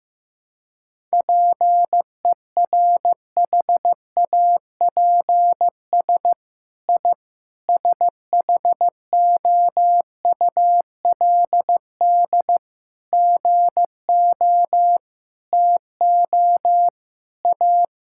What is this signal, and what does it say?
Morse code, heard as a keyed tone, about 15 wpm, 700 Hz. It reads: PERHAPS I SHOULD GO TO A
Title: Sentences from Top 1000 Words - Review Part 5 - 15wpm